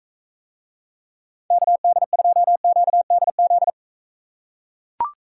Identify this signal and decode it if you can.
Morse code, heard as a keyed tone, about 43 wpm, 700 Hz. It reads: KD2QDZ